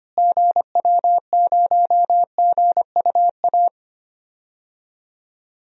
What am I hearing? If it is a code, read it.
Morse code, heard as a keyed tone, about 25 wpm, 700 Hz. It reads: GW0GUA
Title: DXCC-100 Call Signs - 6 Characters - Repeated 2x - 25wpm